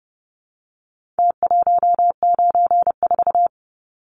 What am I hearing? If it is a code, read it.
Morse code, heard as a keyed tone, about 30 wpm, 700 Hz. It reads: T194
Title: Numbers - Contest Style - 4-Digit 30wpm